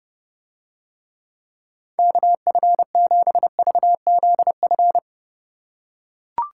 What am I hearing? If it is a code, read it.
Morse code, heard as a keyed tone, about 29 wpm, 700 Hz. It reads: KF7VZF